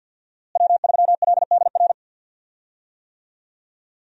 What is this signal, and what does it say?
Morse code, heard as a keyed tone, about 48 wpm, 700 Hz. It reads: W3LDR
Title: USA Call Signs - 5 Characters - Repeated 3x - 50wpm